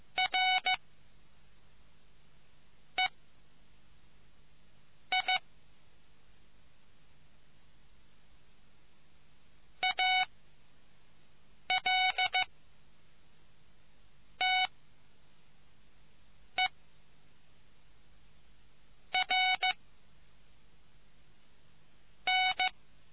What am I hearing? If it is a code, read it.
Morse code, heard as a keyed tone, about 15 wpm, spaced as for 5 wpm, 2250 Hz. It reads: R E I AL T E R N